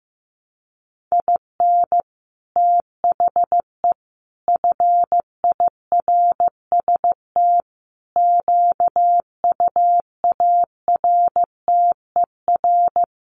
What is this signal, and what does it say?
Morse code, heard as a keyed tone, about 15 wpm, 700 Hz. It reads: IN THE FIRST QUARTER
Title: Sets of 5 Words Encore2 15wpm